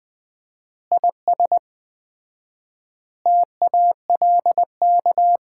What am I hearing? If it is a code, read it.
Morse code, heard as a keyed tone, about 21 wpm, 700 Hz. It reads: IS TALK
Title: 4x Word-Spacing - Sets of 5 Words - 20wpm